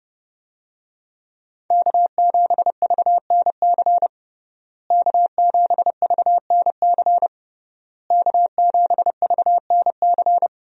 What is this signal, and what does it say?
Morse code, heard as a keyed tone, about 30 wpm, 700 Hz. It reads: K7VNC K7VNC K7VNC
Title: USA Call Signs - 5 Characters - Repeated 3x - 30wpm